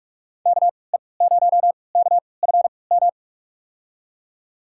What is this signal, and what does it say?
Morse code, heard as a keyed tone, about 46 wpm, spaced as for 25 wpm, 700 Hz. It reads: KE0KFM
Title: USA Call Signs - 6 Characters - 3x Character Spacing - 45wpm